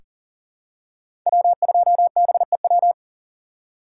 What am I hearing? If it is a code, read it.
Morse code, heard as a keyed tone, about 40 wpm, 700 Hz. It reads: W2BEW
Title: USA Call Signs - 5 Characters - 2x Character Spacing - 40wpm